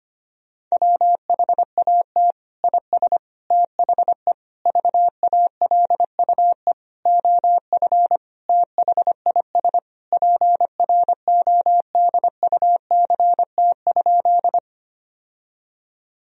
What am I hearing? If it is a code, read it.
Morse code, heard as a keyed tone, about 25 wpm, 700 Hz. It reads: WHAT IS THE VALUE OF THIS PRODUCT?
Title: Sentences from Top 1000 Words - Review Part 3 - 25wpm